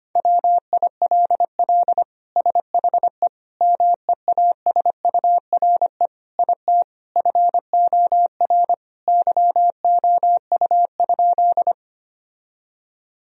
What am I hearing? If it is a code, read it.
Morse code, heard as a keyed tone, about 25 wpm, 700 Hz. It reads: WILL SHE MEASURE IT FOR YOU?
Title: Sentences from Top 1000 Words - Review Part 2 - 25wpm